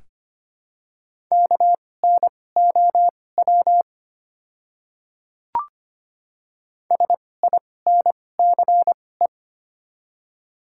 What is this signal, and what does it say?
Morse code, heard as a keyed tone, about 25 wpm, spaced as for 18 wpm, 700 Hz. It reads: KNOW SINCE
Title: 2x Character Spacing - Top 300 Words - 25wpm